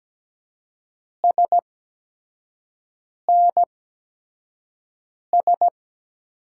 Morse code, heard as a keyed tone, about 17 wpm, spaced as for 5 wpm, 700 Hz. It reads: SNS
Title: USA Call Signs - 6 Characters - 8x Character Spacing - 17wpm